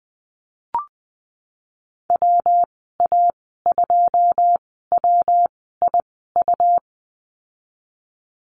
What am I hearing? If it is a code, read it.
Morse code, heard as a keyed tone, about 20 wpm, spaced as for 15 wpm, 700 Hz. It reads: WA2WIU
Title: 140 Lesson 23g - Letter f - Call Signs 20wpm